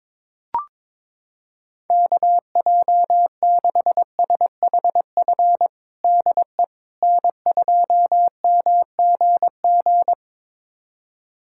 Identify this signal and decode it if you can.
Morse code, heard as a keyed tone, about 22 wpm, 700 Hz. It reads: KJ6SHF DE N2MGG
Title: USA Callsign DE Callsign 22wpm